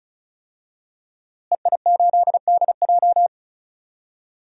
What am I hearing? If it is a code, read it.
Morse code, heard as a keyed tone, about 34 wpm, 700 Hz. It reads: EI8DJ
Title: DXCC-100 Call Signs - 5 Characters - Repeated 2x - 35wpm